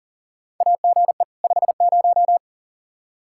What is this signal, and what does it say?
Morse code, heard as a keyed tone, about 40 wpm, 700 Hz. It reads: AGE 50
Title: Practice QSO - Exchange 3 - More Info - 40wpm